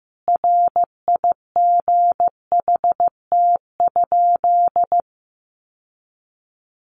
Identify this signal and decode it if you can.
Morse code, heard as a keyed tone, about 15 wpm, 700 Hz. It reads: RIGHT?
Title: Sentences from Top 800 Words 15wpm